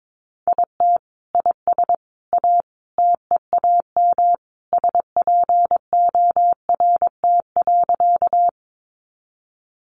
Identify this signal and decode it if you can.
Morse code, heard as a keyed tone, about 22 wpm, 700 Hz. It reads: IT IS A TEAM SPORT.
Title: Sentences from Top 1500 Words 22wpm